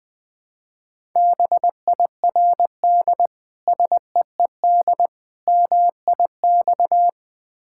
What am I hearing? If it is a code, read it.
Morse code, heard as a keyed tone, about 20 wpm, 700 Hz. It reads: BIRD SEED MIX
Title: Difficult Words with Hints - 20wpm